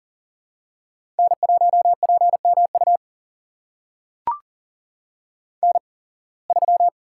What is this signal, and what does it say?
Morse code, heard as a keyed tone, about 40 wpm, spaced as for 26 wpm, 700 Hz. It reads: N1PMU N3
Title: USA Call Signs - 5 Characters - 8x Character Spacing - 40wpm